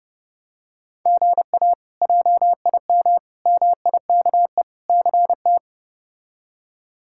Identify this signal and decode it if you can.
Morse code, heard as a keyed tone, about 32 wpm, 700 Hz. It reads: GA JIM MIKE CT
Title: K1USN SST QSO Practice - 30wpm